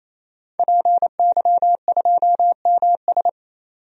Morse code, heard as a keyed tone, about 28 wpm, 700 Hz. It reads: PY2MS
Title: DXCC-100 Call Signs - 5 Characters - 28wpm